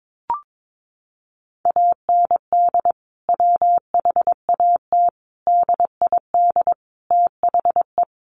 Morse code, heard as a keyed tone, about 22 wpm, 700 Hz. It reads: AND WHAT DID THE
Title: Sentences from Top 1500 Words 22wpm